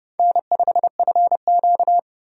Morse code, heard as a keyed tone, about 30 wpm, 700 Hz. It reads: N5FQ
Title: USA Call Signs - 4 Characters - 30wpm